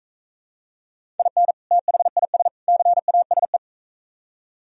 Morse code, heard as a keyed tone, about 45 wpm, 700 Hz. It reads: IN THIS CASE